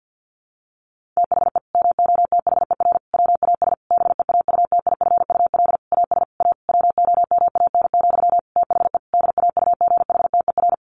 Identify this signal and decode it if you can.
Morse code, heard as a keyed tone, about 50 wpm, 700 Hz. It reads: THE MOTHER WAS BEAUTIFUL AS A WOMAN, THE DAUGHTER
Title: Sentences from Top 1500 Words 50wpm